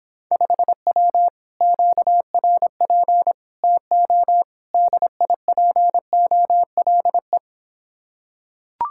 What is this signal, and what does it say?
Morse code, heard as a keyed tone, about 26 wpm, 700 Hz. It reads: 5W QRP TO DIPOLE